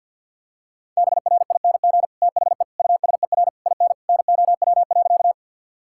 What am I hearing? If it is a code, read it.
Morse code, heard as a keyed tone, about 50 wpm, 700 Hz. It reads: BRING THE USER IN NOW.
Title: Sentences from Top 1000 Words - Review Part 5 - 50wpm